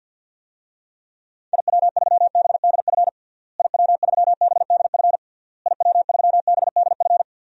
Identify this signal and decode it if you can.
Morse code, heard as a keyed tone, about 50 wpm, 700 Hz. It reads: IW3BDF IW3BDF IW3BDF
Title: DXCC-100 Call Signs - 6 Characters - Repeated 3x - 50wpm